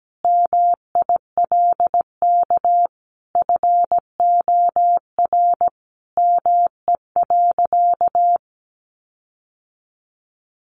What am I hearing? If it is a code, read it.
Morse code, heard as a keyed tone, about 17 wpm, 700 Hz. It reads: MILK FOR ME.
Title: Sentences from Top 1500 Words 17wpm